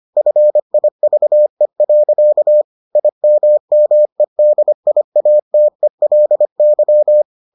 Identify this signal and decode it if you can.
Morse code, heard as a keyed tone, about 25 wpm, 600 Hz. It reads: FIVE. IMMEDIATELY